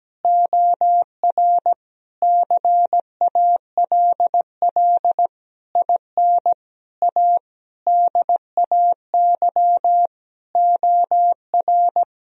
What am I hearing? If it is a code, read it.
Morse code, heard as a keyed tone, about 17 wpm, 700 Hz. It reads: OR CALL IN A DAY OR
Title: Sentences from Top 1000 Words - Review Part 1 - 17wpm